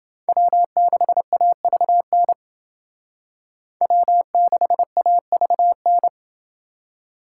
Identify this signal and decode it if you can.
Morse code, heard as a keyed tone, about 29 wpm, 700 Hz. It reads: W6AVN W6AVN